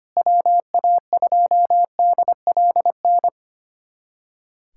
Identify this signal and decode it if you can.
Morse code, heard as a keyed tone, about 25 wpm, 700 Hz. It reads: WA2DLN
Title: USA Callsign DE Callsign - 6 Characters - 25wpm